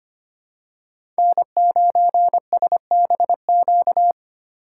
Morse code, heard as a keyed tone, about 25 wpm, 700 Hz. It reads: N9SBQ
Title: USA Call Signs - 5 Characters - 4x Character Spacing - 25wpm